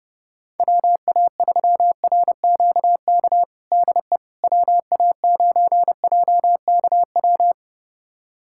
Morse code, heard as a keyed tone, about 32 wpm, 700 Hz. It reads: WA3RQK DE WA9JKW
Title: USA Callsign DE Callsign - 6 Characters - 30wpm